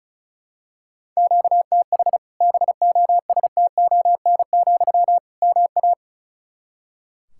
Morse code, heard as a keyed tone, about 36 wpm, 700 Hz. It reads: QTH BOSTON, MA
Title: Practice QSO - Exchange 1 - Beginning - 35wpm